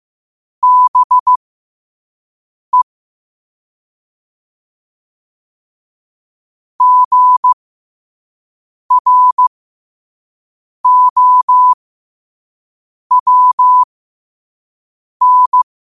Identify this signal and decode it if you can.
Morse code, heard as a keyed tone, about 15 wpm, spaced as for 5 wpm, 1000 Hz. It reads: BE GROWN
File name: qotdfiles / 5wpm